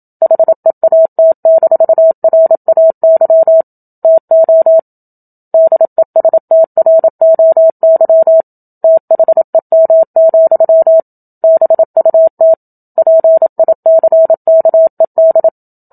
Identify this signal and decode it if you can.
Morse code, heard as a keyed tone, about 27 wpm, 650 Hz. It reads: HEAT-RAY TO DESTROY THEM, BUT PICKED
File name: Book Two / 01-Chapter 1 - 27 WPM